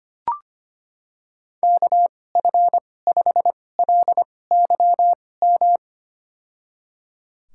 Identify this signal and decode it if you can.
Morse code, heard as a keyed tone, about 25 wpm, spaced as for 18 wpm, 700 Hz. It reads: KF5LYM
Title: USA Call Signs - 2x Character Spacing - 25wpm